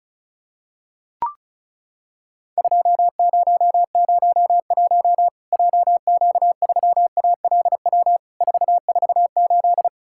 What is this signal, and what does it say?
Morse code, heard as a keyed tone, about 36 wpm, 700 Hz. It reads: 2001 JQ3ALW 448
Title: Worked All Europe DX Contest - QTC Exchange - 35wpm